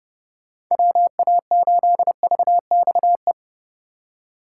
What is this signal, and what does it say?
Morse code, heard as a keyed tone, about 32 wpm, 700 Hz. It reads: WA8VXE